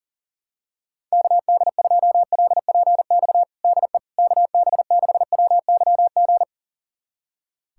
Morse code, heard as a keyed tone, about 40 wpm, 700 Hz. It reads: KD2LPX DE KB6WYG
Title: USA Callsign DE Callsign - 6 Characters - 40wpm